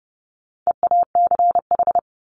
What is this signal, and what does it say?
Morse code, heard as a keyed tone, about 30 wpm, 700 Hz. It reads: EACH
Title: Top 200 Words 30wpm